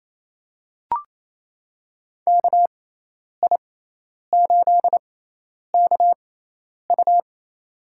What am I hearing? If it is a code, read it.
Morse code, heard as a keyed tone, about 28 wpm, spaced as for 10 wpm, 700 Hz. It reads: KI8KU
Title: USA Call Signs - 5 Characters - 6x Character Spacing - 28wpm